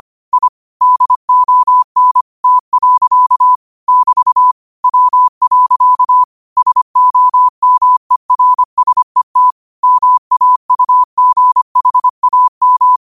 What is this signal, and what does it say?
Morse code, heard as a keyed tone, about 25 wpm, 1000 Hz. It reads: I DON T. <BT> W. SOMERSET MAUGHAM